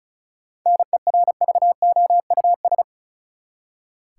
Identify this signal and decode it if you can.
Morse code, heard as a keyed tone, about 34 wpm, 700 Hz. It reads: NERVOUS
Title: Seven Letter Words - 35wpm